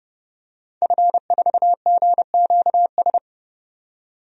Morse code, heard as a keyed tone, about 29 wpm, 700 Hz. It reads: F4GQS
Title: DXCC-100 Call Signs - 5 Characters - 30wpm